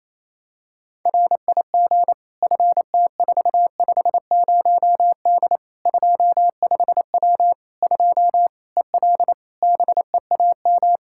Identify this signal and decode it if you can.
Morse code, heard as a keyed tone, about 28 wpm, 700 Hz. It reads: RIG FT450D 25W 2 EL BEAM
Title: Practice QSO - Exchange 2 - Info - 28wpm